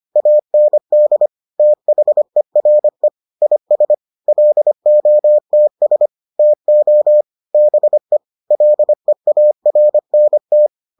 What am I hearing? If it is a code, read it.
Morse code, heard as a keyed tone, about 25 wpm, 600 Hz. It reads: AND THERE IS LOTS TO BE LEARNT